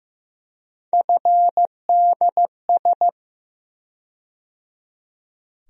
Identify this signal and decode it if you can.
Morse code, heard as a keyed tone, about 15 wpm, 700 Hz. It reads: FDS
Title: USA Call Sign Postfixes - 3 Letters 15wpm